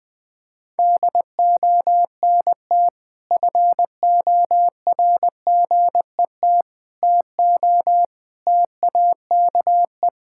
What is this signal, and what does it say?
Morse code, heard as a keyed tone, about 20 wpm, 700 Hz. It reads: DONT FORGET TO TAKE
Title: Sentences with 5-Word Phrases - Spoken Beginning - 20wpm